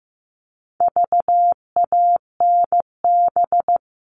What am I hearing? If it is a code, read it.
Morse code, heard as a keyed tone, about 15 wpm, 700 Hz. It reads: VANB
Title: All Cut Numbers - 4-Digit 15wpm